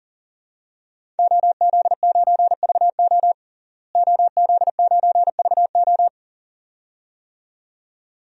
Morse code, heard as a keyed tone, about 40 wpm, 700 Hz. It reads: OZ9VO OZ9VO
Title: DXCC-100 Call Signs - 5 Characters - Repeated 2x - 40wpm